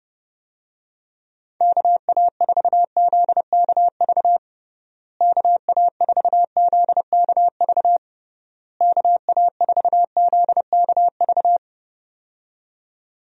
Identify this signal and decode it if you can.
Morse code, heard as a keyed tone, about 31 wpm, 700 Hz. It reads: KA4ZKV KA4ZKV KA4ZKV
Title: USA Call Signs - 6 Characters - Repeated 3x - 30wpm